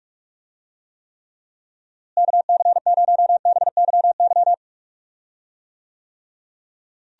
Morse code, heard as a keyed tone, about 46 wpm, 700 Hz. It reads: KC0BYY
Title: USA Call Signs - 6 Characters - Repeated 2x - 45wpm